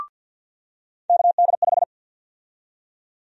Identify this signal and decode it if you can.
Morse code, heard as a keyed tone, about 52 wpm, 700 Hz. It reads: KD5
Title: USA Call Sign Prefixes - 50wpm